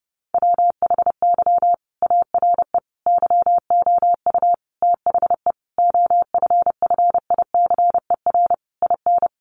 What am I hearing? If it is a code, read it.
Morse code, heard as a keyed tone, about 30 wpm, 700 Hz. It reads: WHY ARE YOU THE OFFICER IN
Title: Sentences from Top 1200 Words 30wpm